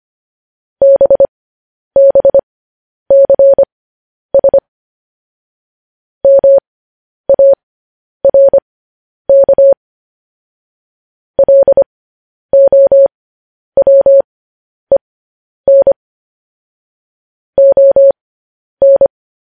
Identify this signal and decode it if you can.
Morse code, heard as a keyed tone, about 25 wpm, spaced as for 10 wpm, 550 Hz. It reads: BBCS MARK LOWEN ON
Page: news-headlines-in-morse-code-at-10-wpm